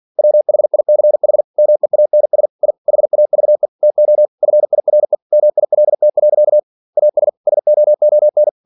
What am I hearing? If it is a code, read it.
Morse code, heard as a keyed tone, about 46 wpm, 600 Hz. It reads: WHICH MEANS I HAVE TO FIRE MILT. AS SOON
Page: Never 47wpm